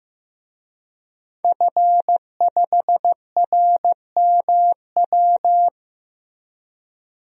Morse code, heard as a keyed tone, about 15 wpm, 700 Hz. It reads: F5RMW
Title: DXCC-100 Call Signs - 5 Characters - Repeated 3x - 15wpm